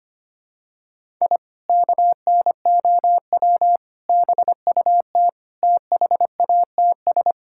Morse code, heard as a keyed tone, about 25 wpm, 700 Hz. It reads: I KNOW BUT THATS
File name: practice / 25wpm / Sentences with 2-Word Phrases - 25wpm